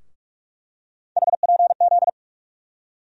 Morse code, heard as a keyed tone, about 42 wpm, 700 Hz. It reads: HPZ
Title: USA Call Sign Postfixes - 3 Letters 45wpm